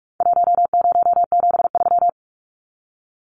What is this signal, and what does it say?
Morse code, heard as a keyed tone, about 44 wpm, 700 Hz. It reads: 1073
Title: Numbers - 4-Digit 45wpm